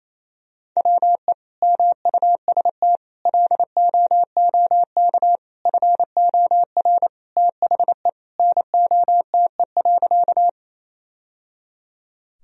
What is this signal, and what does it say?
Morse code, heard as a keyed tone, about 27 wpm, 700 Hz. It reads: WE MUST LOOK FOR THE NOTE.
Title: Sentences from Top 1000 Words - Review Part 2 - 28wpm